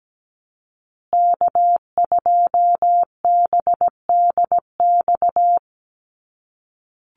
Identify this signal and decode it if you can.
Morse code, heard as a keyed tone, about 17 wpm, 700 Hz. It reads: K2BDX
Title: USA Call Signs Repeated 2x 17wpm